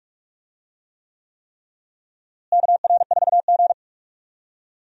Morse code, heard as a keyed tone, about 45 wpm, 700 Hz. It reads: KR4G
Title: USA Call Signs - 4 Characters - Repeated 3x - 45wpm